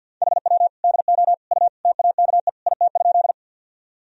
Morse code, heard as a keyed tone, about 51 wpm, 700 Hz. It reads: HW DO U TAKE IT?
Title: Sentences from Top 200 Words and Abbreviations 50wpm